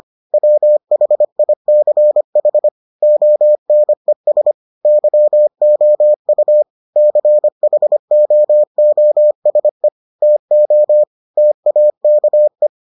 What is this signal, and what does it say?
Morse code, heard as a keyed tone, about 25 wpm, 600 Hz. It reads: WHICH ONES YOU CHOOSE TO TAKE